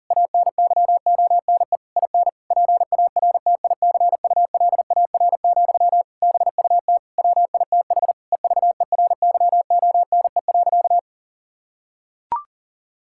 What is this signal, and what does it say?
Morse code, heard as a keyed tone, about 40 wpm, 700 Hz. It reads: ANYONE IN PARTICULAR, BUT WITH EVERYONE.
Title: Sentences from Top 1000 Words - Review Part 5 - 40wpm